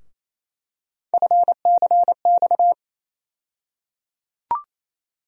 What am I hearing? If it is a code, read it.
Morse code, heard as a keyed tone, about 28 wpm, 700 Hz. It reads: FCX
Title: USA Call Sign Postfixes - 3 Letters 28wpm